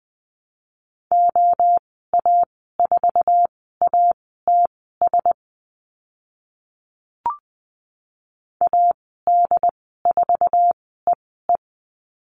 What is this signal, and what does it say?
Morse code, heard as a keyed tone, about 20 wpm, spaced as for 15 wpm, 700 Hz. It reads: OA4ATS AD4EE
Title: 076 Lesson 13g - Letter L - Call Signs 20wpm